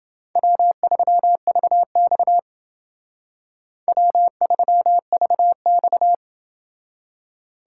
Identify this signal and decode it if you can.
Morse code, heard as a keyed tone, about 31 wpm, 700 Hz. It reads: W3VX W3VX